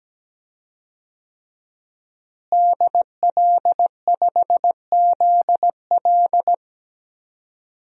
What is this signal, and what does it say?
Morse code, heard as a keyed tone, about 17 wpm, 700 Hz. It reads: DL5ZL
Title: DXCC-100 Call Signs - 5 Characters - Repeated 3x - 17wpm